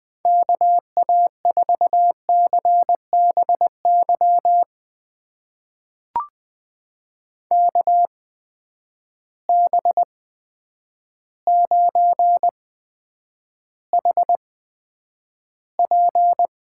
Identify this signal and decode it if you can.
Morse code, heard as a keyed tone, about 20 wpm, spaced as for 9 wpm, 700 Hz. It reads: KA4CBY KB9HP
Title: USA Call Signs - 6 Characters - 8x Character Spacing - 20wpm